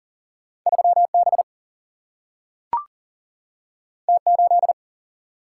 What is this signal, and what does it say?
Morse code, heard as a keyed tone, about 40 wpm, 700 Hz. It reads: 3 B T 8
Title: Single-Letter Single-Number 40wpm